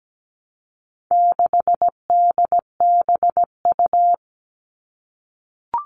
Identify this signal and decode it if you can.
Morse code, heard as a keyed tone, about 17 wpm, 700 Hz. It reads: 6DBU
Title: All Cut Numbers - 4-Digit 17wpm